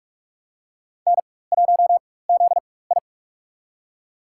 Morse code, heard as a keyed tone, about 47 wpm, spaced as for 21 wpm, 700 Hz. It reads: N1ZI
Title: USA Call Signs - 4x Character Spacing - 45wpm